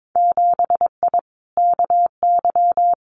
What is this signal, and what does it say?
Morse code, heard as a keyed tone, about 22 wpm, 700 Hz. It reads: 7I KY
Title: Winter Field Day Exchanges Encore 22wpm